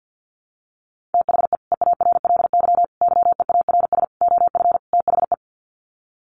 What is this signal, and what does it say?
Morse code, heard as a keyed tone, about 50 wpm, 700 Hz. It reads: THE EARLY YEARS OF THE
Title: Sets of 5 Words Encore 50wpm